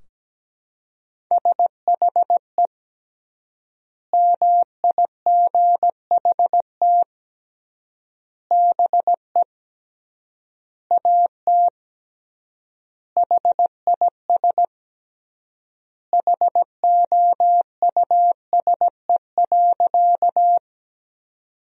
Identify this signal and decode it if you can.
Morse code, heard as a keyed tone, about 17 wpm, 700 Hz. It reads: SHE MIGHT BE AT HIS HOUSE.
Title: 3x Word-Spacing - Sentences from Top 300 Words - 17wpm